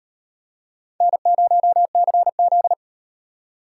Morse code, heard as a keyed tone, about 38 wpm, 700 Hz. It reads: N0CZ